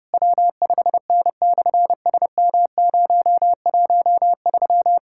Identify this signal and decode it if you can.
Morse code, heard as a keyed tone, about 30 wpm, 700 Hz. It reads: W5N/SM013
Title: SOTA QSO Practice 30wpm